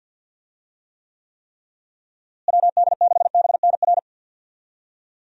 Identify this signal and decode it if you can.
Morse code, heard as a keyed tone, about 52 wpm, 700 Hz. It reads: WD6BNR
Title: USA Call Signs - 6 Characters - Repeated 3x - 50wpm